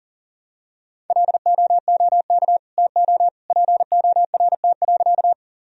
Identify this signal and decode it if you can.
Morse code, heard as a keyed tone, about 40 wpm, 700 Hz. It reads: LOOK TO PORT.
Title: Sentences from Top 1000 Words - Review Part 1 - 40wpm